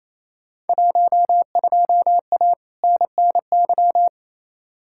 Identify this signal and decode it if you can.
Morse code, heard as a keyed tone, about 28 wpm, 700 Hz. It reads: 12A NNY
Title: Field Day Exchanges Encore - 28wpm